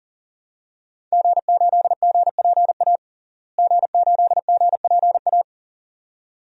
Morse code, heard as a keyed tone, about 40 wpm, 700 Hz. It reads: G8GPA G8GPA
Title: DXCC-100 Call Signs - 5 Characters - Repeated 2x - 40wpm